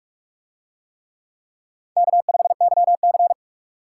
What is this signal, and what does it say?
Morse code, heard as a keyed tone, about 44 wpm, 700 Hz. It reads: K5YC